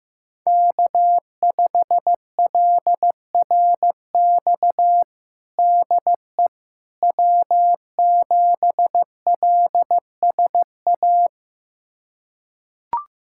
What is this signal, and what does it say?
Morse code, heard as a keyed tone, about 15 wpm, 700 Hz. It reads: K5LRX DE W7LSA
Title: USA Callsign DE Callsign 15wpm